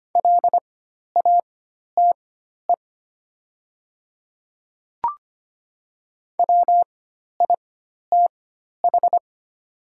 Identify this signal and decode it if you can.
Morse code, heard as a keyed tone, about 25 wpm, spaced as for 12 wpm, 700 Hz. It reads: LATE WITH
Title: 4x Character Spacing - Top 300 Words - 25wpm